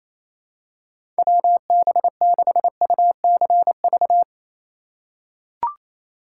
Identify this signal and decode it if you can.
Morse code, heard as a keyed tone, about 28 wpm, 700 Hz. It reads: WB6UCV
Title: USA Call Signs - 2x Character Spacing - 28wpm